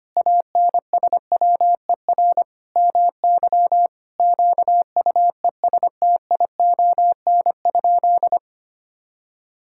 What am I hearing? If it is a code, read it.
Morse code, heard as a keyed tone, about 25 wpm, 700 Hz. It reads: ANSWER MY QUESTION?
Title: Sentences from Top 1000 Words - Review Part 5 - 25wpm